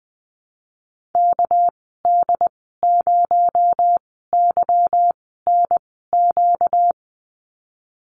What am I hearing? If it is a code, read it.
Morse code, heard as a keyed tone, about 20 wpm, spaced as for 15 wpm, 700 Hz. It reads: KD0YNQ
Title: 238 Lesson 38g - Letter q - Call Signs 20wpm